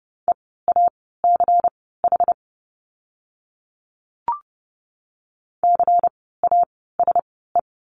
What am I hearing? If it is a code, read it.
Morse code, heard as a keyed tone, about 30 wpm, spaced as for 17 wpm, 700 Hz. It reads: EACH CASE
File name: Lesson 16f - Letter c - Words 30wpm